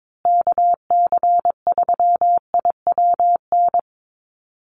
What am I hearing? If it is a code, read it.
Morse code, heard as a keyed tone, about 22 wpm, 700 Hz. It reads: KC3IWN
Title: USA Call Signs Repeated 2x 22wpm